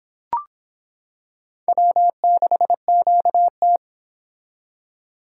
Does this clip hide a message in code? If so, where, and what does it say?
Yes: Morse code from 1.7 to 3.8 s, heard as a keyed tone, about 26 wpm, 700 Hz. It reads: W6QT